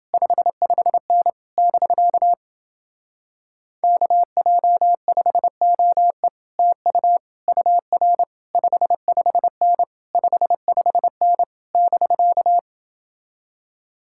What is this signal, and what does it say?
Morse code, heard as a keyed tone, about 29 wpm, 700 Hz. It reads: 55N <BK> KJ5OE TU UR 55N 55N <BK>